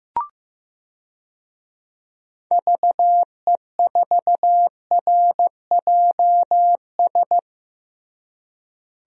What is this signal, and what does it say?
Morse code, heard as a keyed tone, about 15 wpm, 700 Hz. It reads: VE4RJS
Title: DXCC-100 Call Signs - 6 Characters - Repeated 3x - 15wpm